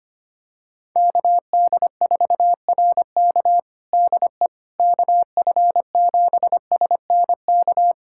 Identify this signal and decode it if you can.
Morse code, heard as a keyed tone, about 25 wpm, 700 Hz. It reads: KD4RK DE KF7SNK
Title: USA Callsign DE Callsign 25wpm